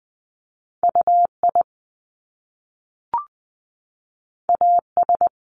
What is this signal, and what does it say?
Morse code, heard as a keyed tone, about 20 wpm, 700 Hz. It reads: U I A S
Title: Top Two Letter Combinations in Words 20wpm